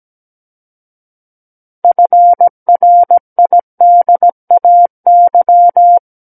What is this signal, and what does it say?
Morse code, heard as a keyed tone, about 17 wpm, 700 Hz. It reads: FRIDAY
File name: Book One / 08-Chapter 8 - 17 WPM